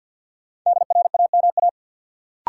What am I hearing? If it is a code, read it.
Morse code, heard as a keyed tone, about 49 wpm, 700 Hz. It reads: DRAMA
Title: Sentences from Top 1500 Words - New Word Drill Alt - 50wpm